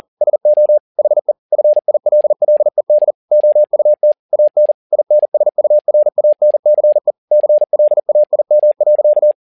Morse code, heard as a keyed tone, about 40 wpm, 600 Hz. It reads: SO HE FILLED OUT AN INSURANCE CLAIM.
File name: learning center / tower trouble / tower touble 40wpm